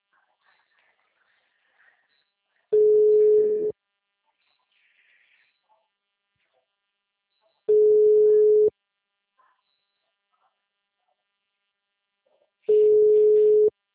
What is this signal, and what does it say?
Ringback tone (Europe, 425 Hz).
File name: out- -2005- - - .27559